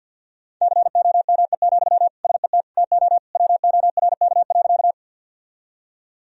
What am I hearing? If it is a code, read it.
Morse code, heard as a keyed tone, about 53 wpm, 700 Hz. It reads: COME, SET TO WORK.
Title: Sentences from Top 1000 Words - Review Part 1 - 50wpm